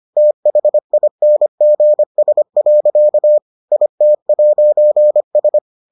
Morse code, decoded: THINGS. IT'S

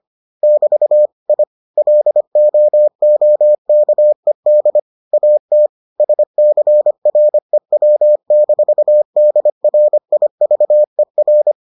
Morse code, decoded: <BT> I LOOKED AT SCREW-DRIVER

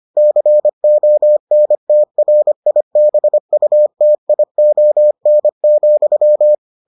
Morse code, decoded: CONTRIBUTION,

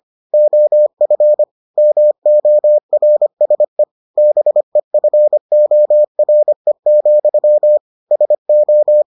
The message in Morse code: OF MORSE BEFORE, SO

